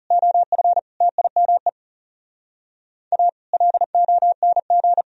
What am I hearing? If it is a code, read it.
Morse code, heard as a keyed tone, about 37 wpm, 700 Hz. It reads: OF TIME A LONG
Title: Speed-Racing - Sets of 5 Words 30wpm